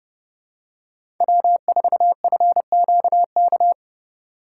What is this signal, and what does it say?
Morse code, heard as a keyed tone, about 30 wpm, 700 Hz. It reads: W4FQK
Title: USA Call Signs - 5 Characters - 4x Character Spacing - 30wpm